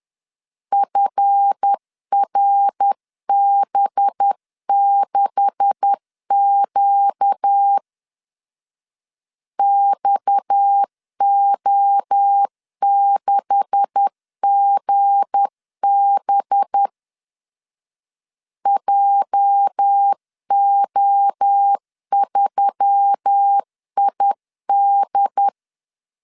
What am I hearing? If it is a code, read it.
Morse code, heard as a keyed tone, about 10 wpm, 800 Hz. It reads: FRB6Q XO6GB JO3ID
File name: LEZIONI / 9 286 10wpm 2735